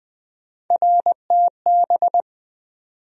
Morse code, heard as a keyed tone, about 21 wpm, 700 Hz. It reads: RTB
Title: USA Call Sign Postfixes - 3 Letters 20wpm